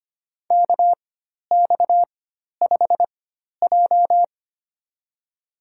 Morse code, heard as a keyed tone, about 25 wpm, spaced as for 12 wpm, 700 Hz. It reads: KX5J
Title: USA Call Signs - 4x Character Spacing - 25wpm